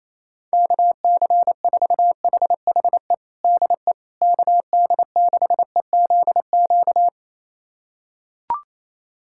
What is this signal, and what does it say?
Morse code, heard as a keyed tone, about 27 wpm, 700 Hz. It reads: KC4HHE DE KD6EZQ
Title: USA Callsign DE Callsign - 6 Characters - 28wpm